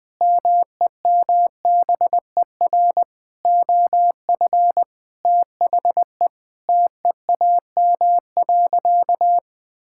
Morse code, heard as a keyed tone, about 20 wpm, 700 Hz. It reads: MEMBER OF THE TEAM.
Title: Sentences with 5-Word Phrases - 20wpm